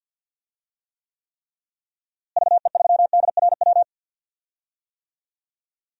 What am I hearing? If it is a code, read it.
Morse code, heard as a keyed tone, about 52 wpm, 700 Hz. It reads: VE3DRW
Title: DXCC-100 Call Signs - 6 Characters - Repeated 3x - 50wpm